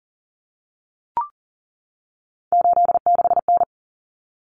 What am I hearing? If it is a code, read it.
Morse code, heard as a keyed tone, about 40 wpm, 700 Hz. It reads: 86N
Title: Common Cut Numbers - 3-Digit 40wpm